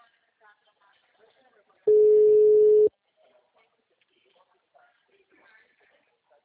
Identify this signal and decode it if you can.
Ringback tone (Europe, 425 Hz), with speech.
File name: out- -2033- - - .36584